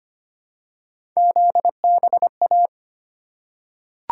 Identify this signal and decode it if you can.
Morse code, heard as a keyed tone, about 25 wpm, 700 Hz. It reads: ZBA
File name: Three Letter-Number Combinations - 25wpm